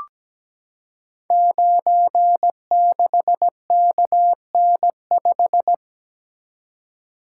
Morse code, heard as a keyed tone, about 17 wpm, 700 Hz. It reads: 96KN5
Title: Code Groups - 5 Letter:Numbers - 17wpm